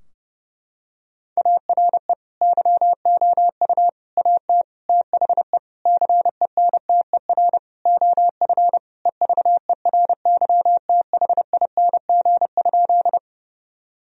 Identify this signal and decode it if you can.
Morse code, heard as a keyed tone, about 30 wpm, 700 Hz. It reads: ARE YOU AT THE CENTER OF EVERYTHING?
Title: Sentences from Top 1000 Words - Review Part 2 - 30wpm